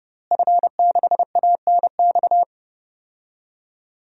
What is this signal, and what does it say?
Morse code, heard as a keyed tone, about 30 wpm, 700 Hz. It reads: F6ANX
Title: DXCC-100 Call Signs - 5 Characters - Repeated 2x - 30wpm